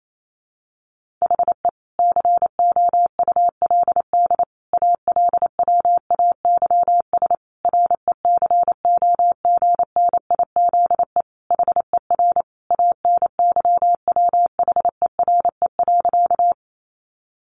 Morse code, heard as a keyed tone, about 28 wpm, 700 Hz. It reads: HE COULD ALWAYS RECOGNIZE HER ANYWHERE.
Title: Sentences from Top 1400 Words 28wpm